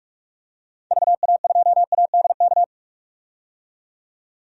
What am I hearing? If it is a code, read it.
Morse code, heard as a keyed tone, about 47 wpm, 700 Hz. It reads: VA2ADK
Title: DXCC-100 Call Signs - 6 Characters - 45wpm